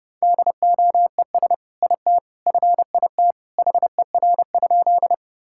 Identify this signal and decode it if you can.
Morse code, heard as a keyed tone, about 32 wpm, 700 Hz. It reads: DOES IT FIT HER?
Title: Sentences from Top 1000 Words - Review Part 5 - 30wpm